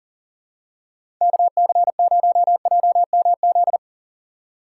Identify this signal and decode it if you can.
Morse code, heard as a keyed tone, about 38 wpm, 700 Hz. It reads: KC0JMZ
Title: USA Call Signs - 6 Characters - 3x Character Spacing - 40wpm